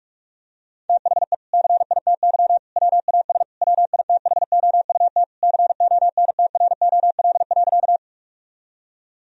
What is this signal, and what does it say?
Morse code, heard as a keyed tone, about 46 wpm, 700 Hz. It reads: THE CITY WAS WITHOUT CONTROL.
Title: Sentences from Top 1000 Words - Review Part 3 - 45wpm